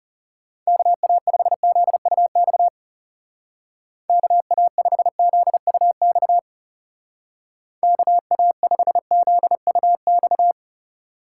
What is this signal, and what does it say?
Morse code, heard as a keyed tone, about 38 wpm, 700 Hz. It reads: KA5ZUX KA5ZUX KA5ZUX